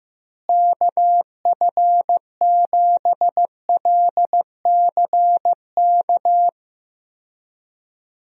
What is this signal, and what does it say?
Morse code, heard as a keyed tone, about 15 wpm, 700 Hz. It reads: KF7LCK
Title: USA Call Signs - 6 Characters - Repeated 3x - 15wpm